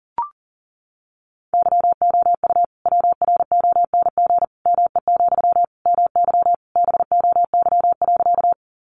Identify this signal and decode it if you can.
Morse code, heard as a keyed tone, about 40 wpm, 700 Hz. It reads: YOU WRONG ME, MY BOY.
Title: Sentences from Top 900 Words 40wpm